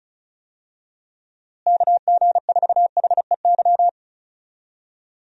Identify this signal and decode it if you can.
Morse code, heard as a keyed tone, about 36 wpm, 700 Hz. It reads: KG4HEY
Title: USA Call Signs - 6 Characters - Repeated 3x - 35wpm